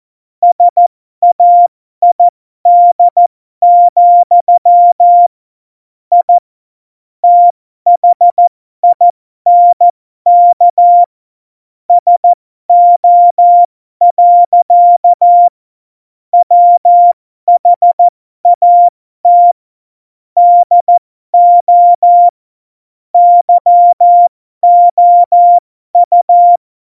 Morse code, heard as a keyed tone, about 14 wpm, spaced as for 12 wpm, 700 Hz. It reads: SAID, I THINK SO. WHAT DO YOU
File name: Morsekurs / Witze